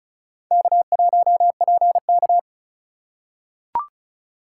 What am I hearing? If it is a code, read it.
Morse code, heard as a keyed tone, about 34 wpm, 690 Hz. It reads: K1PK
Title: USA Call Signs - 4 Characters - 2x Character Spacing - 35wpm